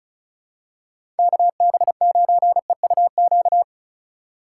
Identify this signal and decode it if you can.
Morse code, heard as a keyed tone, about 34 wpm, 700 Hz. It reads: KB9EUQ
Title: USA Call Signs - 6 Characters - 3x Character Spacing - 35wpm